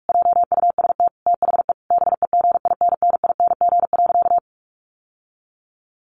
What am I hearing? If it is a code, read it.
Morse code, heard as a keyed tone, about 44 wpm, 700 Hz. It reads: JUST THE BEGINNING.
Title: Sentences from Top 1500 Words 45wpm